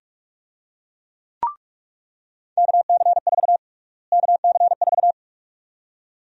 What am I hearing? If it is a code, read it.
Morse code, heard as a keyed tone, about 46 wpm, 700 Hz. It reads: KC4 KC4
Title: USA Call Sign Prefixes - Repeated 2x - 45wpm